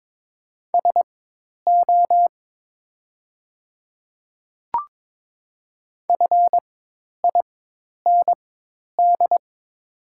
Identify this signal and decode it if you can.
Morse code, heard as a keyed tone, about 22 wpm, spaced as for 10 wpm, 700 Hz. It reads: SO FIND
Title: 4x Character Spacing - Top 100 Words - 22wpm